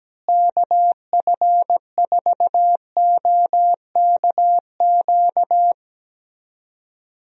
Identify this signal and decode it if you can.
Morse code, heard as a keyed tone, about 17 wpm, 700 Hz. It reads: KF4OKQ
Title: USA Call Signs - 6 Characters - Repeated 3x - 17wpm